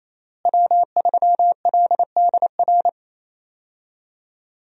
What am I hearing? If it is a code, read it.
Morse code, heard as a keyed tone, about 27 wpm, 700 Hz. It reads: W3LDR
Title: USA Call Signs - 5 Characters - Repeated 3x - 28wpm